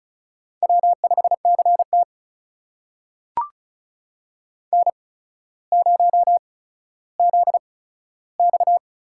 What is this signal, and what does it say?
Morse code, heard as a keyed tone, about 36 wpm, spaced as for 15 wpm, 700 Hz. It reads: W5CT N0ZX
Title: USA Call Signs - 4 Characters - 8x Character Spacing - 35wpm